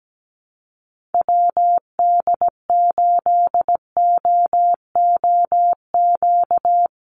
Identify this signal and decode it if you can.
Morse code, heard as a keyed tone, about 17 wpm, 700 Hz. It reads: WD8OOQ
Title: USA Call Signs Repeated 2x 17wpm